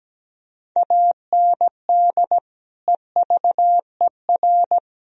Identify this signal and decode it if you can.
Morse code, heard as a keyed tone, about 17 wpm, 700 Hz. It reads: AND EVER
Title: Sentences from Top 1000 Words - Review Part 1 - 17wpm